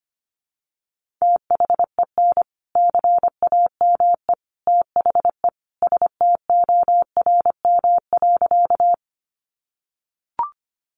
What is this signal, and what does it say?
Morse code, heard as a keyed tone, about 25 wpm, 700 Hz. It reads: THEN CAME THE STORM.
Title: Sentences from Top 1500 Words 25wpm